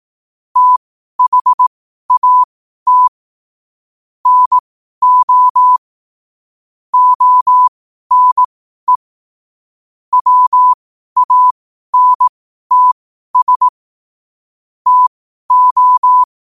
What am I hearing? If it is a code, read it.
Morse code, heard as a keyed tone, about 18 wpm, spaced as for 12 wpm, 1000 Hz. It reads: THAT NO ONE WANTS TO
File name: qotdfiles / 12wpm